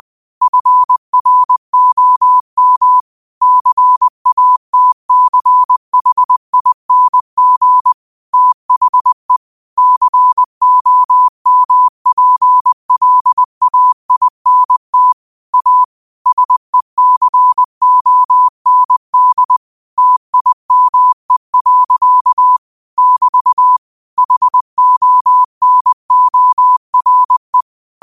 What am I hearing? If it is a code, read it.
Morse code, heard as a keyed tone, about 20 wpm, 1000 Hz. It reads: FROM CATCHING THE COMPLAINT A SECOND TIME. <BT> HONORE
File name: qotdfiles / 20wpm